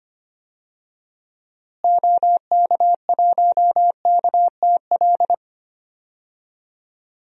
Morse code, heard as a keyed tone, about 25 wpm, 700 Hz. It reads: OK1KTL
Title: DXCC-100 Call Signs - 6 Characters - Repeated 2x - 25wpm